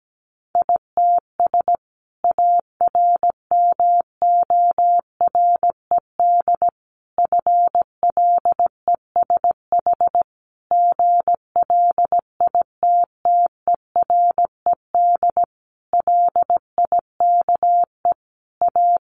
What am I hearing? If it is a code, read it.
Morse code, heard as a keyed tone, about 17 wpm, 700 Hz. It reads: ITS ARMORED FLESH GLITTERED LIKE A
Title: A Sound Of Thunder 17wpm